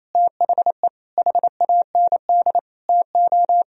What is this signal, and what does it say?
Morse code, heard as a keyed tone, about 28 wpm, 700 Hz. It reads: THE HAND TO
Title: Sentences from Top 1000 Words - Review Part 2 - 28wpm